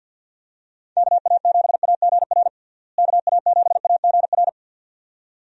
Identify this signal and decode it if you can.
Morse code, heard as a keyed tone, about 52 wpm, 700 Hz. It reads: KA7AGR KA7AGR
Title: USA Call Signs - 6 Characters - Repeated 2x - 50wpm